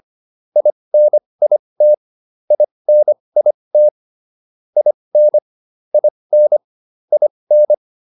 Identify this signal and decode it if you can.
Morse code, heard as a keyed tone, about 25 wpm, spaced as for 20 wpm, 600 Hz. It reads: INIT INIT IN IN IN